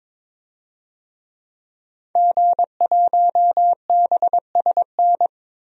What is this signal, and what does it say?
Morse code, heard as a keyed tone, about 22 wpm, 700 Hz. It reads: G1BSN
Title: DXCC-100 Call Signs - 5 Characters - 22wpm